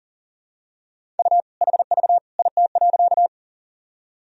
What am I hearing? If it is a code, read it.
Morse code, heard as a keyed tone, about 40 wpm, 700 Hz. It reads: U HV IT.
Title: Sentences from Top 100 Words and Abbreviations 40wpm